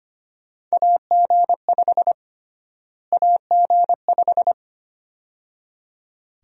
Morse code, heard as a keyed tone, about 25 wpm, 700 Hz. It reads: AG5 AG5
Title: USA Call Sign Prefixes - Repeated 2x - 25wpm